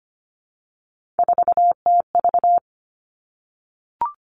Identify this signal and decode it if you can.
Morse code, heard as a keyed tone, about 25 wpm, 700 Hz. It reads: 4TV
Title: Three Challenging Letter-Number Combinations 25wpm